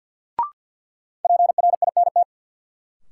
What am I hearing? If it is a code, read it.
Morse code, heard as a keyed tone, about 47 wpm, 700 Hz. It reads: PRINT
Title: Sentences from Top 1400 Words - New Word Drill - 50wpm